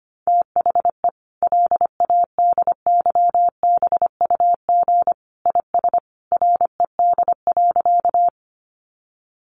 Morse code, heard as a keyed tone, about 25 wpm, 700 Hz. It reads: THE LADYBUG IS RED.
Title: Additional Easy Phrases 25wpm